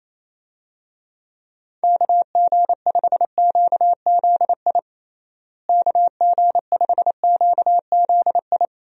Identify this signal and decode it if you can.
Morse code, heard as a keyed tone, about 27 wpm, 700 Hz. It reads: KG5QZI KG5QZI
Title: USA Call Signs - 6 Characters - Repeated 3x - 28wpm